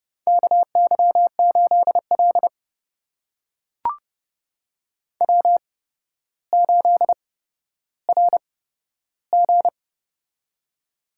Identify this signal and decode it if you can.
Morse code, heard as a keyed tone, about 28 wpm, spaced as for 13 wpm, 700 Hz. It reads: KY8L W8RG
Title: USA Call Signs - 4 Characters - 8x Character Spacing - 30wpm